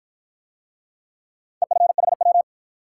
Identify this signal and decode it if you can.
Morse code, heard as a keyed tone, about 54 wpm, 700 Hz. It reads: EFHW